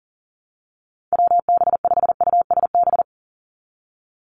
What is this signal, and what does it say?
Morse code, heard as a keyed tone, about 40 wpm, 700 Hz. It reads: WB5USB
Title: NA SOTA Activator Callsigns 40wpm